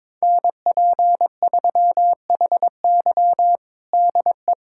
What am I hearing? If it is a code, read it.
Morse code, heard as a keyed tone, about 22 wpm, 700 Hz. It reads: NP3HY DE
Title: USA Callsign DE Callsign - 5 Characters - 22wpm